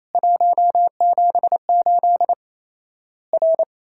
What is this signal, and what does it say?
Morse code, heard as a keyed tone, about 28 wpm, 700 Hz. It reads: 178 R
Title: Worked All Europe DX Contest - QTC Exchange - 28wpm